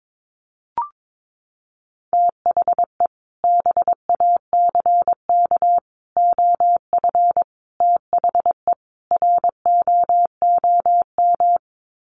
Morse code, heard as a keyed tone, about 22 wpm, 700 Hz. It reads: THE BACK OF THE ROOM
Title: Sets of 5 Words Encore 22wpm